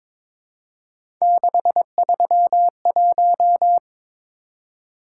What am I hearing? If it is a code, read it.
Morse code, heard as a keyed tone, about 22 wpm, 700 Hz. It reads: 631
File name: practice / 22wpm / Three Letter-Number Combinations - 22wpm